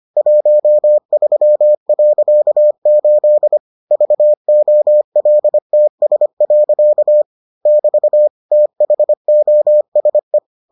Morse code, heard as a keyed tone, about 25 wpm, 600 Hz. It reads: 13.8 VOLTS. <BT> THOSE